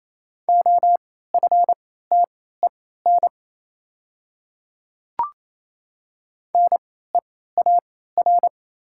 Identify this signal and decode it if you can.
Morse code, heard as a keyed tone, about 27 wpm, spaced as for 16 wpm, 700 Hz. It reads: OFTEN NEAR
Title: 3x Character Spacing - Top 300 Words - 28wpm